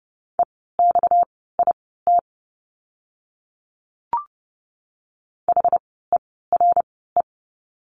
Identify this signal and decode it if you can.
Morse code, heard as a keyed tone, about 30 wpm, spaced as for 17 wpm, 700 Hz. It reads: EXIT HERE
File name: course / 30:10wpm / 230 Lesson 37f - Letter x - Words 30wpm